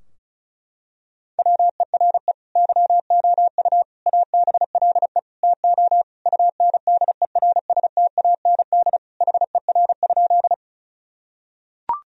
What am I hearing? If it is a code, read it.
Morse code, heard as a keyed tone, about 34 wpm, 700 Hz. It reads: WERE YOU ABLE TO UNDERSTAND HER?
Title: Sentences from Top 1000 Words - Review Part 2 - 35wpm